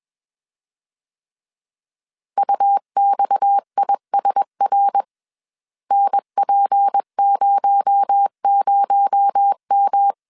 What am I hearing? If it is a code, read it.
Morse code, heard as a keyed tone, about 21 wpm, 800 Hz. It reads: UXISR NP00M
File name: CW / DET / 1 6 1500C S B 20wpm 2048